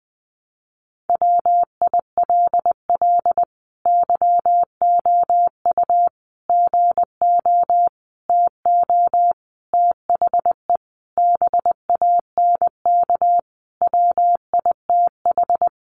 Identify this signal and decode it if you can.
Morse code, heard as a keyed tone, about 20 wpm, 700 Hz. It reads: WILL YOU GO TO THE BANK WITH
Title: Sentences from Top 900 Words 20wpm